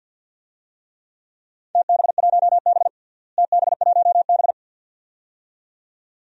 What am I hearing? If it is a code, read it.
Morse code, heard as a keyed tone, about 51 wpm, 700 Hz. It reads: TB1B TB1B
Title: DXCC-100 Call Signs - 4 Characters - Repeated 2x - 50wpm